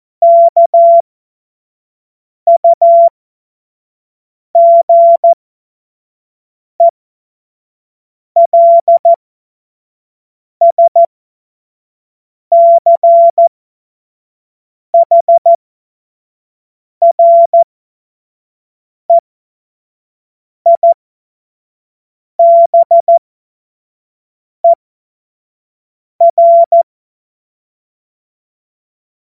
Morse code, heard as a keyed tone, about 14 wpm, spaced as for 5 wpm, 680 Hz. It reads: KUGELSCHREIBER
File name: Morsekurs / Witze / Witz 16 April 2024 25 F70